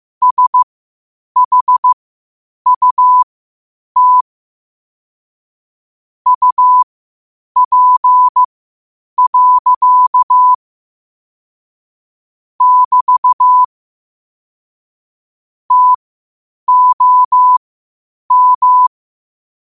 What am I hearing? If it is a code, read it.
Morse code, heard as a keyed tone, about 15 wpm, spaced as for 8 wpm, 1000 Hz. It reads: SHUT UP. <BT> TOM
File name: qotdfiles / 8wpm